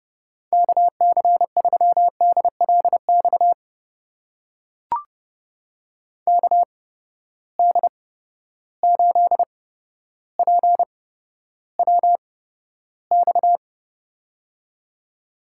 Morse code, heard as a keyed tone, about 30 wpm, spaced as for 13 wpm, 700 Hz. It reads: KC3DLX KD8PWX
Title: USA Call Signs - 6 Characters - 8x Character Spacing - 30wpm